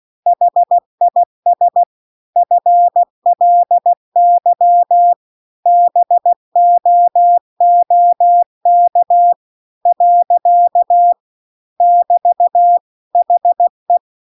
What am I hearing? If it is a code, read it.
Morse code, heard as a keyed tone, about 16 wpm, 700 Hz. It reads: HIS FLY BOOK . <BT> HE
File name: CW-The-Big-Twoheaded-River-2-16WPM